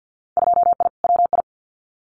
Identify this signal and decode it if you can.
Morse code, heard as a keyed tone, about 50 wpm, 700 Hz. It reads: 2I WI
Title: Winter Field Day Exchanges 50wpm